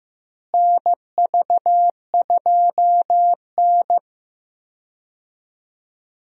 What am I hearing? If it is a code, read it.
Morse code, heard as a keyed tone, about 16 wpm, 700 Hz. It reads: NV2N
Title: USA Call Signs - 4 Characters - Repeated 3x - 15wpm